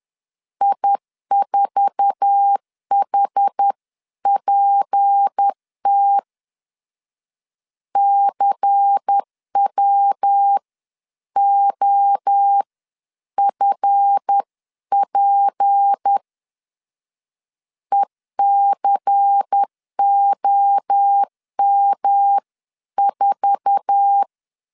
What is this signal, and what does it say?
Morse code, heard as a keyed tone, about 10 wpm, spaced as for 9 wpm, 800 Hz. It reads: I4HPT CWOFP ECOM4
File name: LEZIONI / 8 YJ45 / 8 YJ4 10wpm 2633